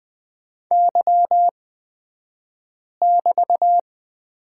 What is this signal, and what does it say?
Morse code, heard as a keyed tone, about 20 wpm, spaced as for 5 wpm, 700 Hz. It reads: Y<BT>
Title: 249 Lesson 40d - BK - Two Characters 20wpm